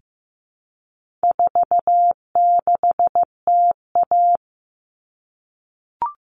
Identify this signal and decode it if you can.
Morse code, heard as a keyed tone, about 15 wpm, 700 Hz. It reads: 46TA
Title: All Cut Numbers - 4-Digit 15wpm